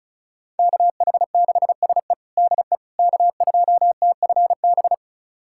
Morse code, heard as a keyed tone, about 36 wpm, 700 Hz. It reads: KH6SE DE K2TFB
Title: USA Callsign DE Callsign - 5 Characters - 35wpm